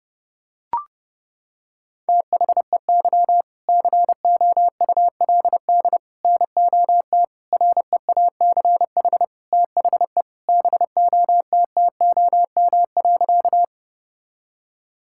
Morse code, heard as a keyed tone, about 32 wpm, 700 Hz. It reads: THEY COULD NOT REACH THE BOTTOM.
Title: Sentences from Top 1000 Words - Review Part 6 - 30wpm